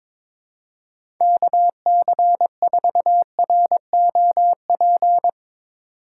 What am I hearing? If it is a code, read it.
Morse code, heard as a keyed tone, about 22 wpm, 700 Hz. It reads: KC4ROP
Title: USA Call Signs - 6 Characters - 2x Character Spacing - 22wpm